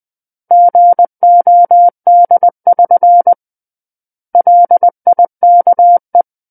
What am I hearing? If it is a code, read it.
Morse code, heard as a keyed tone, about 20 wpm, 700 Hz. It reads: GOD<SN> LIKE
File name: 20 WPM